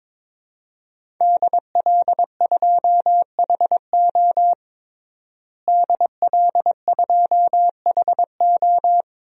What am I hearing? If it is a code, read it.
Morse code, heard as a keyed tone, about 21 wpm, 700 Hz. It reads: DL2HO DL2HO
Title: DXCC-100 Call Signs - 5 Characters - Repeated 2x - 22wpm